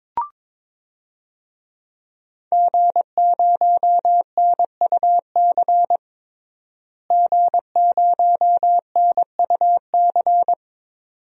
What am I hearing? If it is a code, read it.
Morse code, heard as a keyed tone, about 22 wpm, 700 Hz. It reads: G0NUC G0NUC
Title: DXCC-100 Call Signs - 5 Characters - Repeated 2x - 22wpm